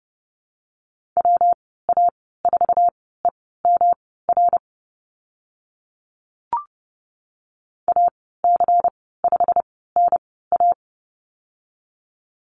Morse code, heard as a keyed tone, about 30 wpm, spaced as for 17 wpm, 700 Hz. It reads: WA4EMR AC5NA
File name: Lesson 19g - Letter w - Call Signs 30wpm